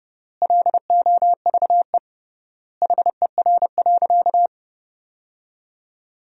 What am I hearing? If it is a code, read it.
Morse code, heard as a keyed tone, about 32 wpm, 700 Hz. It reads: LOVE HER.
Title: 3x Word-Spacing - Sentences from Top 400 Words - 30wpm